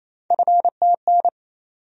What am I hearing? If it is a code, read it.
Morse code, heard as a keyed tone, about 28 wpm, 700 Hz. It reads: FTN
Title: USA Call Sign Postfixes - 3 Letters 28wpm